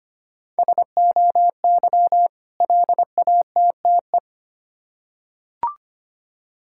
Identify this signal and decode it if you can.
Morse code, heard as a keyed tone, about 25 wpm, 700 Hz. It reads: SOY LATTE
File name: practice / 25wpm / Difficult Words with Hints - 25wpm